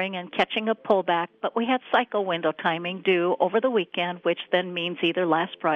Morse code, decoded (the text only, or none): none